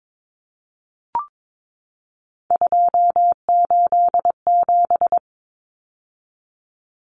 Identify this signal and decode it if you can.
Morse code, heard as a keyed tone, about 22 wpm, 700 Hz. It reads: 287
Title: Numbers - 3-Digit 22wpm